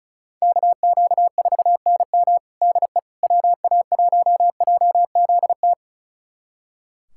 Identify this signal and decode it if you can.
Morse code, heard as a keyed tone, about 36 wpm, 700 Hz. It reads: KQ4NM DE WA1JZT
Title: USA Callsign DE Callsign 35wpm